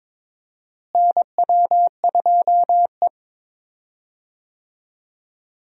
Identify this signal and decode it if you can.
Morse code, heard as a keyed tone, about 21 wpm, 700 Hz. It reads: NW2E
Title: USA Call Signs - 4 Characters - Repeated 3x - 22wpm